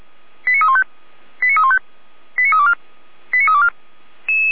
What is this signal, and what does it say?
Five-tone selective call (ZVEI), heard as quick runs of five tones, tones of about 70 ms. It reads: 89315 89315 89324 89324